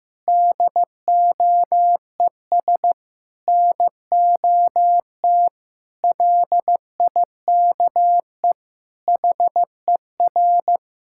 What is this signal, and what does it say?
Morse code, heard as a keyed tone, about 16 wpm, 700 Hz. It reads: DOES NOT LIKE HER
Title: Sentences from Top 1000 Words - Review Part 5 - 15wpm